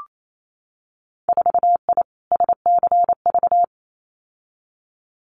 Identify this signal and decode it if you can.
Morse code, heard as a keyed tone, about 28 wpm, 700 Hz. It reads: 4I SCV
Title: Winter Field Day Exchanges Encore 28wpm